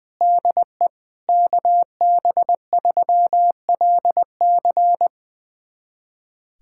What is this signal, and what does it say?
Morse code, heard as a keyed tone, about 20 wpm, 700 Hz. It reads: DE KB3LC
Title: USA Callsign DE Callsign - 5 Characters - 20wpm